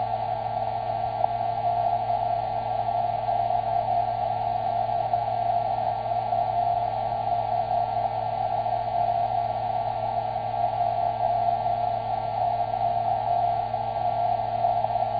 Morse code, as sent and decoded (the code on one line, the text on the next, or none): none